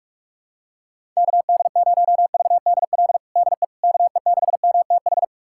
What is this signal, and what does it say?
Morse code, heard as a keyed tone, about 46 wpm, 700 Hz. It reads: KD0VDL DE KE6MTH
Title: USA Callsign DE Callsign - 6 Characters - 45wpm